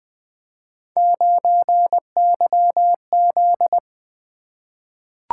Morse code, heard as a keyed tone, about 20 wpm, 700 Hz. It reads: 9YZ
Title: Three Letter-Number Combinations - 20wpm